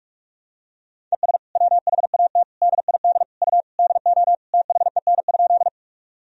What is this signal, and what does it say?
Morse code, heard as a keyed tone, about 46 wpm, 700 Hz. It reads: ES WHAT DID U DO THEN?
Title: Sentences from Top 100 Words and Abbreviations 45wpm